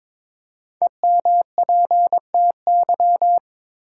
Morse code, heard as a keyed tone, about 22 wpm, 700 Hz. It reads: EMPTY
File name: 5 Letter Words - First 3 Letters Given - 22wpm